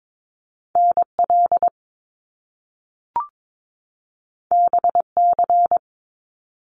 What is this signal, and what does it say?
Morse code, heard as a keyed tone, about 22 wpm, 700 Hz. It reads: NL BC
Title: Canadian Provinces and Territories 22wpm